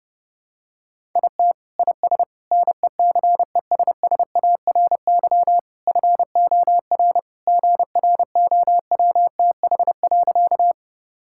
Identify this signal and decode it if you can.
Morse code, heard as a keyed tone, about 30 wpm, 700 Hz. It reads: IT IS NECESSARY FOR GROWTH.
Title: Sentences from Top 1000 Words - Review Part 5 - 30wpm